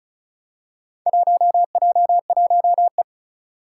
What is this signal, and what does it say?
Morse code, heard as a keyed tone, about 36 wpm, 700 Hz. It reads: 1J1E
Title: Four Challenging Letter-Number Combinations 35wpm